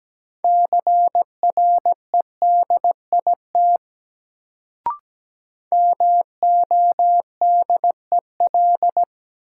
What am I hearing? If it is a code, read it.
Morse code, heard as a keyed tone, about 17 wpm, 700 Hz. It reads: CREDIT MODEL
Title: Sentences from Top 1000 Words - New Word Drill - 17wpm